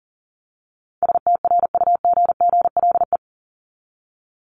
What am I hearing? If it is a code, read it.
Morse code, heard as a keyed tone, about 40 wpm, 700 Hz. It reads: STRUGGLE
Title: Top 1751-2000 Words 40wpm